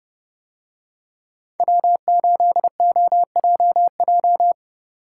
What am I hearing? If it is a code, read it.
Morse code, heard as a keyed tone, about 30 wpm, 700 Hz. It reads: W8OJJ